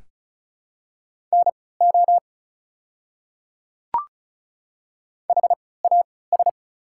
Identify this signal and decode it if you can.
Morse code, heard as a keyed tone, about 34 wpm, spaced as for 20 wpm, 700 Hz. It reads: NO HAS
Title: 3x Character Spacing - Top 300 Words - 35wpm